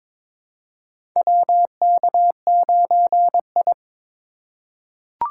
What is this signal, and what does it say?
Morse code, heard as a keyed tone, about 22 wpm, 700 Hz. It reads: WK9I
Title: USA Call Signs - 4 Characters - 4x Character Spacing - 22wpm